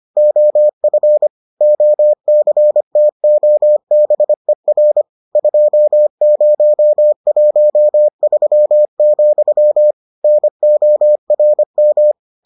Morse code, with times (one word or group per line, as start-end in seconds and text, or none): 0.2-1.3 OF
1.6-5.0 OCTOBER
5.4-9.9 2013,
10.2-12.1 NORM